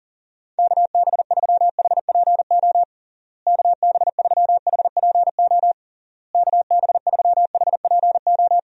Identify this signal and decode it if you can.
Morse code, heard as a keyed tone, about 40 wpm, 700 Hz. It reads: KB3HPO KB3HPO KB3HPO
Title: USA Call Signs - 6 Characters - Repeated 3x - 40wpm